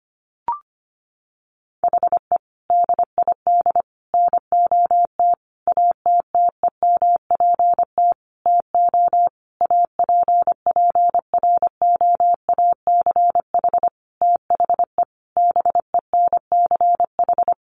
Morse code, heard as a keyed tone, about 25 wpm, 700 Hz. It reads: HE DID NOT ATTEMPT TO APPROACH THE BENCH
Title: Sentences from Top 1400 Words 25wpm